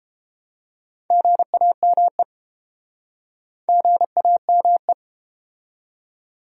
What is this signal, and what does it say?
Morse code, heard as a keyed tone, about 31 wpm, 700 Hz. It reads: GAME GAME